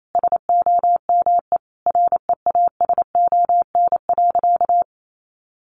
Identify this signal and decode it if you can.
Morse code, heard as a keyed tone, about 28 wpm, 700 Hz. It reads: SOME REASON.
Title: Sentences from Top 700 Words 28wpm